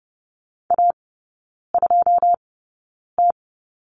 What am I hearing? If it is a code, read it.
Morse code, heard as a keyed tone, about 30 wpm, spaced as for 9 wpm, 700 Hz. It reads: A2T
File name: 081 Lesson 14e - Number 2 - Three Characters 30wpm